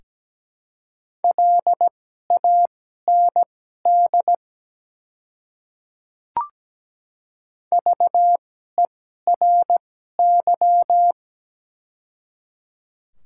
Morse code, heard as a keyed tone, about 17 wpm, spaced as for 12 wpm, 700 Hz. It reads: LAND VERY
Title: 2x Character Spacing - Top 200 Words - 17wpm